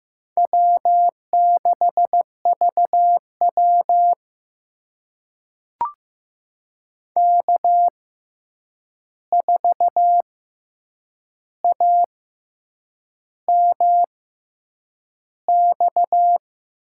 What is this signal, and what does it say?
Morse code, heard as a keyed tone, about 15 wpm, spaced as for 7 wpm, 700 Hz. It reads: W6VW K4AMX
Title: USA Call Signs - 6x Character Spacing - 15wpm